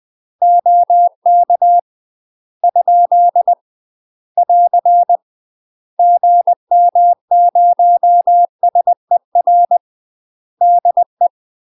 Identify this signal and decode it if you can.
Morse code, heard as a keyed tone, about 20 wpm, 700 Hz. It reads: OK ? <AR> GM0SER DE
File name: UK MASTER QSOs TRAINING / 20WPM / UK MASTER QSOs 0043